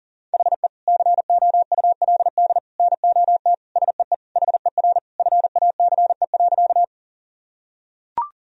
Morse code, heard as a keyed tone, about 40 wpm, 700 Hz. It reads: HE COULD NOT SEE HER FACE.
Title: Sentences from Top 1000 Words - Review Part 2 - 40wpm